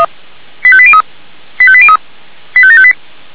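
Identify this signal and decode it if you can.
Five-tone selective call (ZVEI), heard as quick runs of five tones, tones of about 70 ms. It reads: 85803 85803 85758
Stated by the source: ZVEI five-tone selective call received on an SDR.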